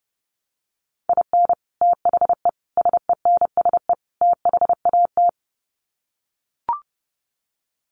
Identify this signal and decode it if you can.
Morse code, heard as a keyed tone, about 30 wpm, 700 Hz. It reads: IN THE SENSE THAT
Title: Sets of 4 Words 30wpm